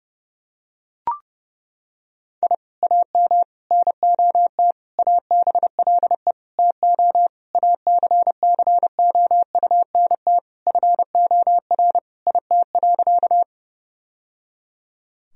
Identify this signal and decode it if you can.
Morse code, heard as a keyed tone, about 30 wpm, 700 Hz. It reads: I AM NOT ABLE TO ACCOUNT FOR IT.
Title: Sentences from Top 1000 Words - Review Part 2 - 30wpm